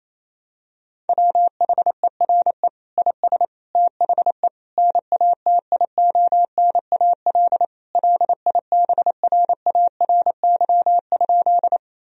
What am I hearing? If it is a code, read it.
Morse code, heard as a keyed tone, about 28 wpm, 700 Hz. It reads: WHERE IS THE NATIONAL LIBRARY?
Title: Sentences from Top 1000 Words - Review Part 4 - 28wpm